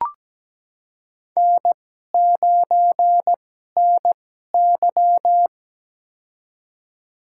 Morse code, heard as a keyed tone, about 17 wpm, spaced as for 12 wpm, 700 Hz. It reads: N9NY
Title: USA Call Signs - 2x Character Spacing - 17wpm